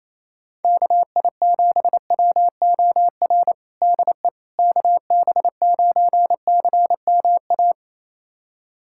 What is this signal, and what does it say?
Morse code, heard as a keyed tone, about 28 wpm, 700 Hz. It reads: KI7WOR DE KB9CMA
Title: USA Callsign DE Callsign - 6 Characters - 28wpm